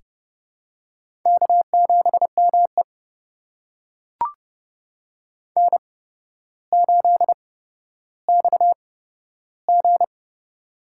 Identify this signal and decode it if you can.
Morse code, heard as a keyed tone, about 31 wpm, spaced as for 13 wpm, 700 Hz. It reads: K7ME N8XG
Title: USA Call Signs - 4 Characters - 8x Character Spacing - 30wpm